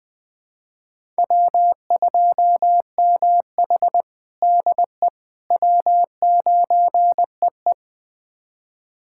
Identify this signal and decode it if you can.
Morse code, heard as a keyed tone, about 20 wpm, 700 Hz. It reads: W2MH DE W9EE
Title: USA Callsign DE Callsign - 4 Characters - 20wpm